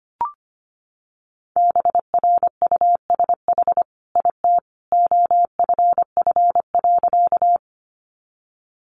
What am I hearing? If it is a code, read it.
Morse code, heard as a keyed tone, about 25 wpm, 700 Hz. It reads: BRUSH IT OFF.
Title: Sentences from Top 1100 Words 25wpm